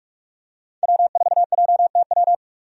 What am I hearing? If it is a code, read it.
Morse code, heard as a keyed tone, about 46 wpm, 700 Hz. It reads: W4JTW
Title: USA Call Signs - 5 Characters - 45wpm